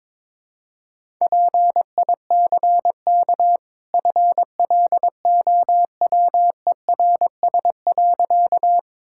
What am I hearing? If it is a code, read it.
Morse code, heard as a keyed tone, about 21 wpm, 700 Hz. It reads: PICK FLOWERS.
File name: practice / 22wpm / Easy Sentences with Hints - 22wpm